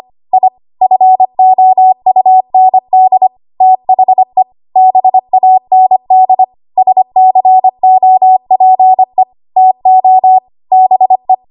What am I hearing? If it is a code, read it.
Morse code, heard as a keyed tone, about 25 wpm, 750 Hz. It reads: I FOUND THE BAND SCOPE TO BE
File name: Archive / 25 WPM / 25WPM